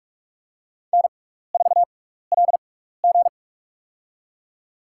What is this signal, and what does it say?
Morse code, heard as a keyed tone, about 45 wpm, spaced as for 15 wpm, 700 Hz. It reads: N4LG
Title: USA Call Signs - 4 Characters - 6x Character Spacing - 45wpm